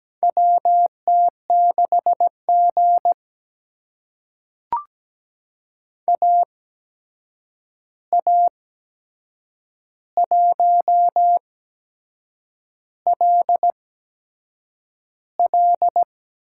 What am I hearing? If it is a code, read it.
Morse code, heard as a keyed tone, about 17 wpm, spaced as for 7 wpm, 700 Hz. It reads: WT6G AA1LL
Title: USA Call Signs - 8x Character Spacing - 17wpm